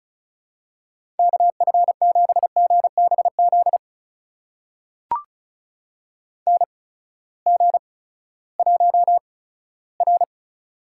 Morse code, heard as a keyed tone, about 36 wpm, spaced as for 18 wpm, 700 Hz. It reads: KF7GBZ NG1R
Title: USA Call Signs - 8x Character Spacing - 35wpm